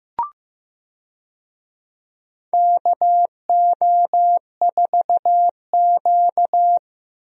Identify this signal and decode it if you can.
Morse code, heard as a keyed tone, about 15 wpm, 700 Hz. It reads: KO4Q
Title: USA Call Signs - 4 Characters - Repeated 3x - 15wpm